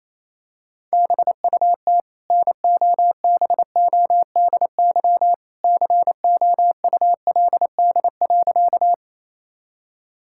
Morse code, heard as a keyed tone, about 28 wpm, 700 Hz. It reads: BUT NOBODY COULD.
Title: Sentences from Top 1600 Words 28wpm